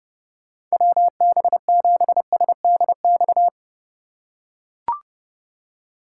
Morse code, heard as a keyed tone, about 30 wpm, 700 Hz. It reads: WB7SDX